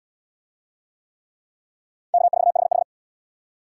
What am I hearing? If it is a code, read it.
Morse code, heard as a keyed tone, about 75 wpm, 700 Hz. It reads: K4HH